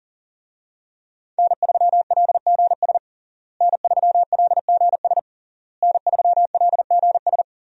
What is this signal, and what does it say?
Morse code, heard as a keyed tone, about 40 wpm, 700 Hz. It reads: N3LGS N3LGS N3LGS
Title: USA Call Signs - 5 Characters - Repeated 3x - 40wpm